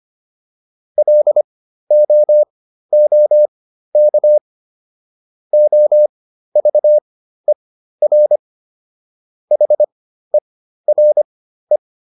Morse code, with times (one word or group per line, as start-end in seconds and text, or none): 1.0-4.4 LOOK
5.5-8.4 OVER
9.5-11.8 HERE